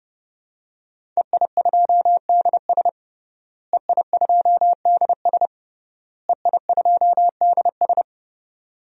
Morse code, heard as a keyed tone, about 31 wpm, 700 Hz. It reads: EI2DS EI2DS EI2DS
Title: DXCC-100 Call Signs - 5 Characters - Repeated 3x - 30wpm